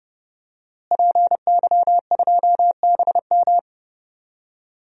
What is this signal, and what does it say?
Morse code, heard as a keyed tone, about 30 wpm, 700 Hz. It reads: PY2BM
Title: DXCC-100 Call Signs - 5 Characters - Repeated 3x - 30wpm